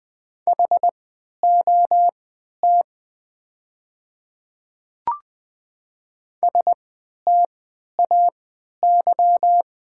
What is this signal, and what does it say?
Morse code, heard as a keyed tone, about 20 wpm, spaced as for 11 wpm, 700 Hz. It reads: HOT STAY
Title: 3x Character Spacing - Top 500 Words - 20wpm